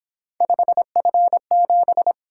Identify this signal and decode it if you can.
Morse code, heard as a keyed tone, about 26 wpm, 700 Hz. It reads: 5F7
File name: Speed-Racing - Three Letter-Number Combinations - 22wpm